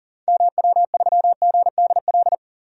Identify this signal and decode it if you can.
Morse code, heard as a keyed tone, about 40 wpm, 700 Hz. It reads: MW3GDL
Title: DXCC-100 Call Signs - 6 Characters - 40wpm